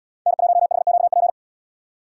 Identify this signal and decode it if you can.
Morse code, heard as a keyed tone, about 77 wpm, 700 Hz. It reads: N1HQP